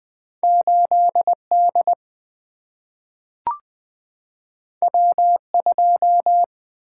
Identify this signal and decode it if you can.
Morse code, heard as a keyed tone, about 20 wpm, 700 Hz. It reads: 8 D W 2
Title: Single-Letter Single-Number 20wpm